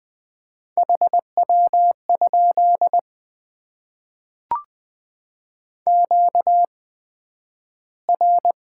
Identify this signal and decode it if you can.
Morse code, heard as a keyed tone, about 20 wpm, spaced as for 11 wpm, 700 Hz. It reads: HW? QR
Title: Top 20 QSO Elements - 8x Character Spacing - 20wpm